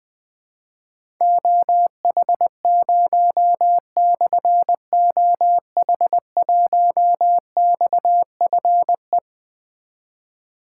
Morse code, heard as a keyed tone, about 20 wpm, 700 Hz. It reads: OH0/OH1XFE
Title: Difficult Call Signs 20wpm